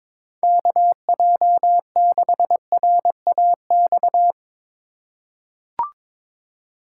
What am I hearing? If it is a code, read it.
Morse code, heard as a keyed tone, about 22 wpm, 700 Hz. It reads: KJ6RAX
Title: USA Call Signs - 6 Characters - 2x Character Spacing - 22wpm